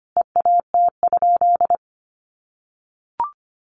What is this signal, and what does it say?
Morse code, heard as a keyed tone, about 25 wpm, 700 Hz. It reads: EAT?
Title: Sentences from Top 500 Words 25wpm